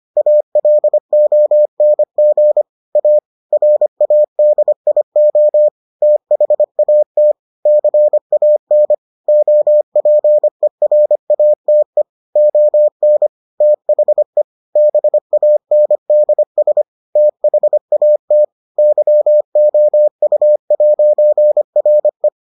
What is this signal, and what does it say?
Morse code, heard as a keyed tone, about 25 wpm, 600 Hz. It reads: ALONG A RADIO THAT CAN OPERATE ON THE BANDS THAT YOU'RE